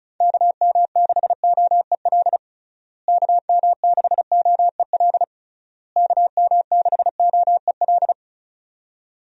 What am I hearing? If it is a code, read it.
Morse code, heard as a keyed tone, about 36 wpm, 700 Hz. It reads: KM6OEL KM6OEL KM6OEL
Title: USA Call Signs - 6 Characters - Repeated 3x - 35wpm